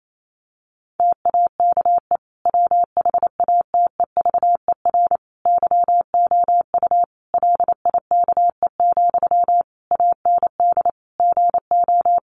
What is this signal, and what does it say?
Morse code, heard as a keyed tone, about 28 wpm, 700 Hz. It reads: TAKE WHATEVER YOU LIKE, AND GO
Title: Sentences from Top 1100 Words 28wpm